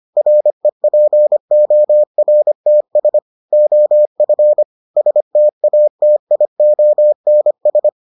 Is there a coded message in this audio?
Yes: Morse code from 0.2 to 7.9 s, heard as a keyed tone, about 25 wpm, 600 Hz. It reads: REPORTS OF STATIONS